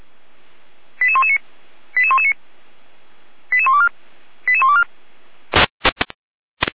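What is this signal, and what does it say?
Five-tone selective call (ZVEI), heard as quick runs of five tones, tones of about 70 ms. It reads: 88119 88119 88124 88124
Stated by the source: ZVEI five-tone selective call 88093 received on an SDR, fire brigade (Feuerwehr) Orlamünde.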